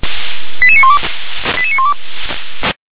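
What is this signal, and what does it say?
Five-tone selective call (ZVEI), heard as quick runs of five tones, tones of about 70 ms. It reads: 88912 88912
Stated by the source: ZVEI five-tone selective call received on SDR.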